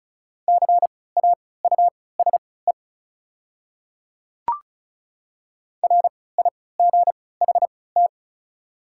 Morse code, heard as a keyed tone, about 36 wpm, spaced as for 20 wpm, 700 Hz. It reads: CAUSE RIGHT
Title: 3x Character Spacing - Top 200 Words - 35wpm